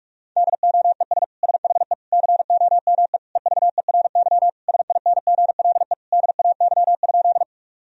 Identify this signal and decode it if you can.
Morse code, heard as a keyed tone, about 46 wpm, 700 Hz. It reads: DOES SHE COME EVERY SINGLE DAY?
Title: Sentences from Top 1000 Words - Review Part 4 - 45wpm